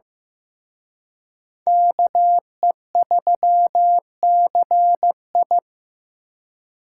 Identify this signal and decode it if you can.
Morse code, heard as a keyed tone, about 18 wpm, 700 Hz. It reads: KE3CI
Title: Speed Racing - USA 2x2 Call Signs 15wpm